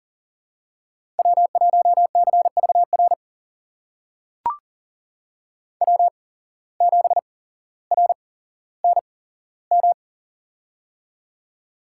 Morse code, heard as a keyed tone, about 39 wpm, spaced as for 17 wpm, 700 Hz. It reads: W1CVR W7RNM
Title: USA Call Signs - 5 Characters - 8x Character Spacing - 40wpm